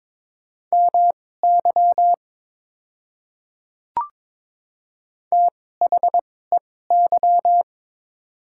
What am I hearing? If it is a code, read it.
Morse code, heard as a keyed tone, about 22 wpm, spaced as for 16 wpm, 690 Hz. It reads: MY THEY
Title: 2x Character Spacing - Top 100 Words - 22wpm